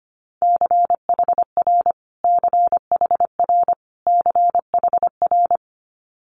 Morse code, heard as a keyed tone, about 25 wpm, 700 Hz. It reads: CHR CHR CHR
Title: Multi-letter Phonemes Repeated 3x 25wpm